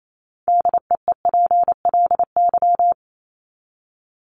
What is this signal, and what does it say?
Morse code, heard as a keyed tone, about 28 wpm, 700 Hz. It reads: DEEPLY
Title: LY Suffix - Top 100 Words with Base Words 28wpm